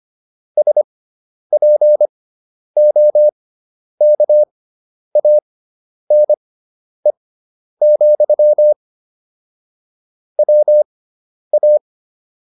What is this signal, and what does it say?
Morse code, heard as a keyed tone, about 25 wpm, spaced as for 10 wpm, 600 Hz. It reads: SPOKANE, WA